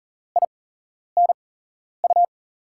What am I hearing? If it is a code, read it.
Morse code, heard as a keyed tone, about 40 wpm, spaced as for 11 wpm, 700 Hz. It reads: INU
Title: USA Call Signs - 6 Characters - 8x Character Spacing - 40wpm